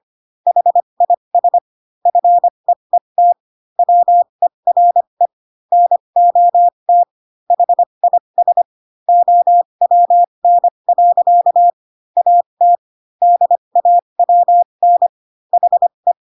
Morse code, heard as a keyed tone, about 25 wpm, spaced as for 22 wpm, 700 Hz. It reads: HIS FEET WERE NOT HIS OWN. AT DAWN HE
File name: Pinocchio / Pinocchio ch6 18 25 ik1hge